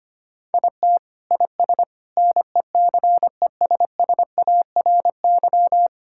Morse code, heard as a keyed tone, about 25 wpm, 700 Hz. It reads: IT IS NECESSARY
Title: Sentences from Top 1000 Words - Review Part 5 - 25wpm